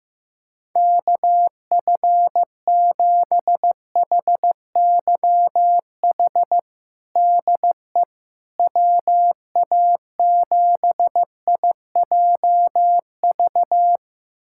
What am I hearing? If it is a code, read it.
Morse code, heard as a keyed tone, about 15 wpm, 700 Hz. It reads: KF7HYH DE WA7IJV
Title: USA Callsign DE Callsign - 6 Characters - 15wpm